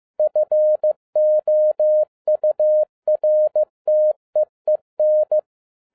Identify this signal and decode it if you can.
Morse code, heard as a keyed tone, about 15 wpm, 610 Hz. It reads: FOURTEEN